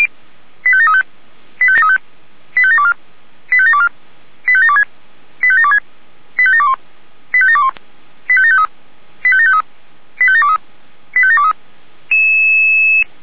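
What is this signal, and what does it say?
Five-tone selective call (ZVEI), heard as quick runs of five tones, tones of about 80 ms. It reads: 85735 85735 85724 85724 85726 85726 85721 85721 85753 85753 85723 85723
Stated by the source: ZVEI five-tone selective call received on SDR.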